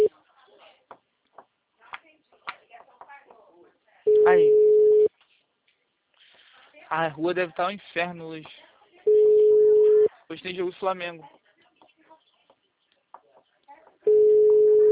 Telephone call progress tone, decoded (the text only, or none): ringback tone